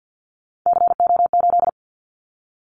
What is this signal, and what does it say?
Morse code, heard as a keyed tone, about 50 wpm, 690 Hz. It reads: CO8
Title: Callsign Prefixes for Top 100 DXCC - 50wpm